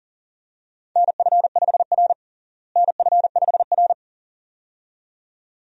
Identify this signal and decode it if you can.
Morse code, heard as a keyed tone, about 40 wpm, 700 Hz. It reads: NF5R NF5R
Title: USA Call Signs - 4 Characters - Repeated 2x - 40wpm